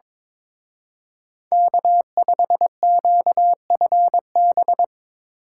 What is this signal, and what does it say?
Morse code, heard as a keyed tone, about 25 wpm, 700 Hz. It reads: K5QFB
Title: Speed Racing - USA Call Signs 22wpm